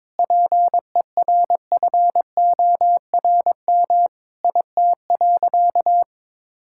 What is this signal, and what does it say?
Morse code, heard as a keyed tone, about 22 wpm, 700 Hz. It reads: PERFORM IT.
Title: Sentences from Top 1000 Words - Review Part 5 - 22wpm